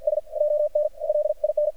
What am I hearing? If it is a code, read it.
Morse code, heard as a keyed tone, about 58 wpm, spaced as for 48 wpm, 600 Hz, with radio noise. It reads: S ON LIN
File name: archived / CRSnet